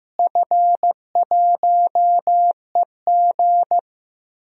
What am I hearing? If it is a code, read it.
Morse code, heard as a keyed tone, about 16 wpm, 700 Hz. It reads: F1EG
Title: DXCC-100 Call Signs - 4 Characters - Repeated 2x - 15wpm